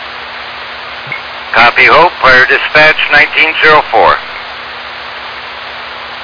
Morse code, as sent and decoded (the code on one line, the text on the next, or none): none